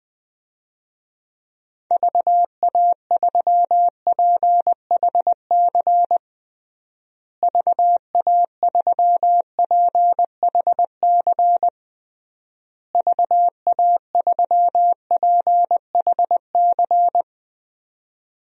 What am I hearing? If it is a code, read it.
Morse code, heard as a keyed tone, about 20 wpm, 700 Hz. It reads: VA3PHC VA3PHC VA3PHC
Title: DXCC-100 Call Signs - 6 Characters - Repeated 3x - 20wpm